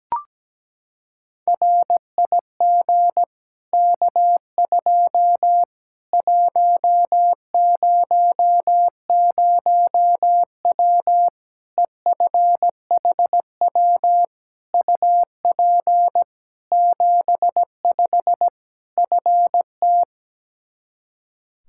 Morse code, heard as a keyed tone, about 17 wpm, 700 Hz. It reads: RIG K2 100W EFHW UP 75 FT
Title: Practice QSO - Exchange 2 - Info - 17wpm